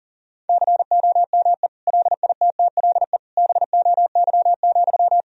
Morse code, heard as a keyed tone, about 40 wpm, 700 Hz. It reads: COME LITTLE BOY,
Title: Sentences from Top 1000 Words - Review Part 2 - 40wpm